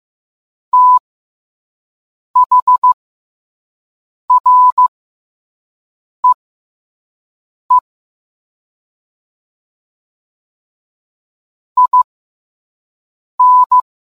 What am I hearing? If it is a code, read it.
Morse code, heard as a keyed tone, about 15 wpm, spaced as for 5 wpm, 1000 Hz. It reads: THREE IN